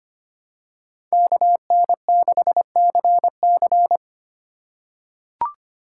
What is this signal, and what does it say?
Morse code, heard as a keyed tone, about 25 wpm, 700 Hz. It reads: KN6CC
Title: USA Call Signs - 5 Characters - 4x Character Spacing - 25wpm